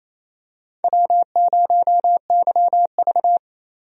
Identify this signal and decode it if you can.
Morse code, heard as a keyed tone, about 27 wpm, 700 Hz. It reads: W0YV
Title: USA Call Signs - 4 Characters - Repeated 2x - 28wpm